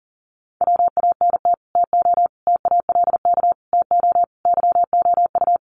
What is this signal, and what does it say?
Morse code, heard as a keyed tone, about 40 wpm, 700 Hz. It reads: WANT TO TALK TO YOU
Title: Sets of 5 Words Encore 40wpm